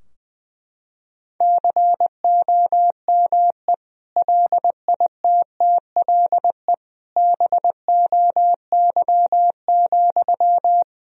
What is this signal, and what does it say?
Morse code, heard as a keyed tone, about 20 wpm, 700 Hz. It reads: COME LITTLE BOY,
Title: Sentences from Top 1000 Words - Review Part 2 - 20wpm